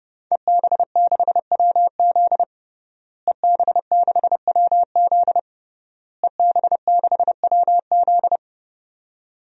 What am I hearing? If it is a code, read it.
Morse code, heard as a keyed tone, about 30 wpm, 700 Hz. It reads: EB6WZ EB6WZ EB6WZ
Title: DXCC-100 Call Signs - 5 Characters - Repeated 3x - 30wpm